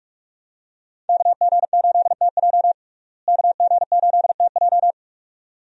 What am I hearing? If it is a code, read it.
Morse code, heard as a keyed tone, about 46 wpm, 700 Hz. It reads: KG8TJ KG8TJ
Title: USA Call Signs - 5 Characters - Repeated 2x - 45wpm